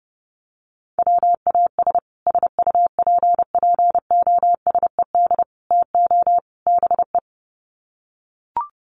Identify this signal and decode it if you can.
Morse code, heard as a keyed tone, about 28 wpm, 700 Hz. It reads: WAS SUPPOSED TO BE
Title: Sets of 4 Words 30wpm